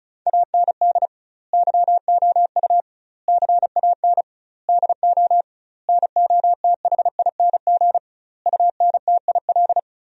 Morse code, heard as a keyed tone, about 34 wpm, 700 Hz. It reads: AND YOU CAN DO NOTHING UNTIL
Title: 2x Word-Spacing - Sentences from Top 500 Words - 35wpm